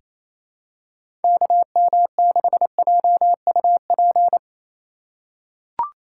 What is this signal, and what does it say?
Morse code, heard as a keyed tone, about 28 wpm, 700 Hz. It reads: KM6JUP
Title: USA Call Signs - 6 Characters - 2x Character Spacing - 28wpm